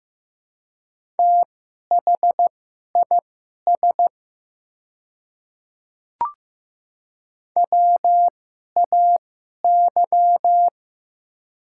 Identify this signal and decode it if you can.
Morse code, heard as a keyed tone, about 15 wpm, spaced as for 11 wpm, 700 Hz. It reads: THIS WAY
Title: 2x Character Spacing - Top 200 Words - 15wpm